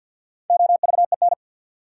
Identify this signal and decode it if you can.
Morse code, heard as a keyed tone, about 51 wpm, 700 Hz. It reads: OVEN
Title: Sentences from Top 1000 Words - Review Part 4 - New Word Drill - 50wpm